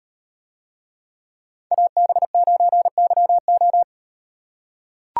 Morse code, heard as a keyed tone, about 38 wpm, 700 Hz. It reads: AB9YO